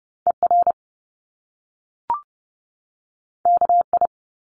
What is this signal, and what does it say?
Morse code, heard as a keyed tone, about 30 wpm, 700 Hz. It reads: ER KI
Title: Top Two Letter Combinations in Words 30wpm